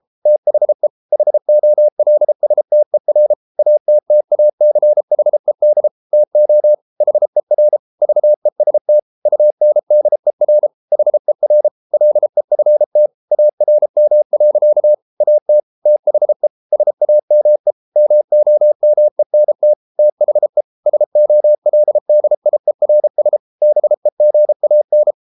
Morse code, heard as a keyed tone, about 33 wpm, 600 Hz. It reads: THE HOLSTER ATTACHED TO HER VEST UNDER HER LEFT ARM. AT THE SAME MOMENT THE SOLDIERS BEGAN